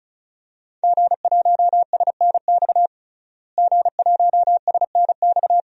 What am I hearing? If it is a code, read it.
Morse code, heard as a keyed tone, about 34 wpm, 700 Hz. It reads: G1SNX G1SNX
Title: DXCC-100 Call Signs - 5 Characters - Repeated 3x - 35wpm